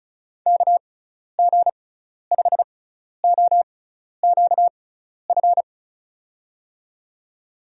Morse code, heard as a keyed tone, about 36 wpm, spaced as for 12 wpm, 700 Hz. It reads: KG5OQF
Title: USA Call Signs - 6 Characters - 6x Character Spacing - 35wpm